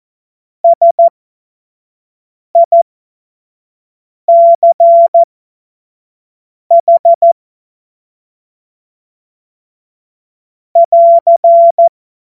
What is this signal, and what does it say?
Morse code, heard as a keyed tone, about 14 wpm, spaced as for 5 wpm, 680 Hz. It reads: SICH <AR>